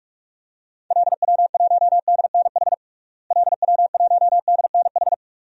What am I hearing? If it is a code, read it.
Morse code, heard as a keyed tone, about 46 wpm, 700 Hz. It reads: LW1DNH LW1DNH
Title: DXCC-100 Call Signs - 6 Characters - Repeated 2x - 45wpm